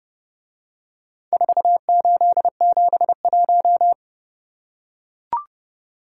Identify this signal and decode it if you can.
Morse code, heard as a keyed tone, about 30 wpm, 700 Hz. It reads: 4871